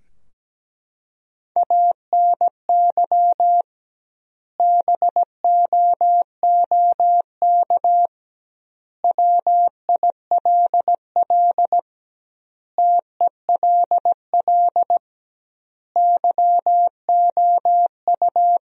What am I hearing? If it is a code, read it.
Morse code, heard as a keyed tone, about 17 wpm, 700 Hz. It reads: ANY BOOK WILL TELL YOU
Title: 2x Word-Spacing - Sentences from Top 300 Words - 17wpm